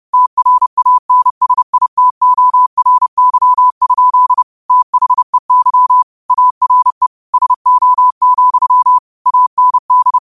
Morse code, heard as a keyed tone, about 30 wpm, 1000 Hz. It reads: TRANSITORY? THEY ARE SO, AND
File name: qotdfiles / 30wpm